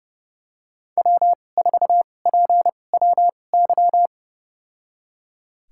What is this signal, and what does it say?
Morse code, heard as a keyed tone, about 30 wpm, spaced as for 22 wpm, 700 Hz. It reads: W4PWY
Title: USA Call Signs - 5 Characters - 2x Character Spacing - 30wpm